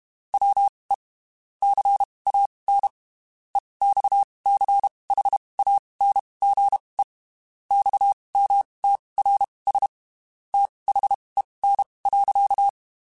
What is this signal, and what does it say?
Morse code, heard as a keyed tone, about 32 wpm, spaced as for 23 wpm, 800 Hz. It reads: WE CAN EXCHANGE XMTRS THEN.
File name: (spaced) / QSOs Spaced-mp3s / WA7QOX 1 @30 sp